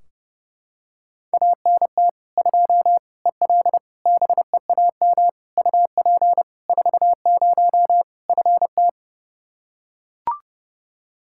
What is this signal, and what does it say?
Morse code, heard as a keyed tone, about 32 wpm, 700 Hz. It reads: ANT 2 EL BEAM UP 40 FT
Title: Practice QSO - Exchange 2 - Info - 30wpm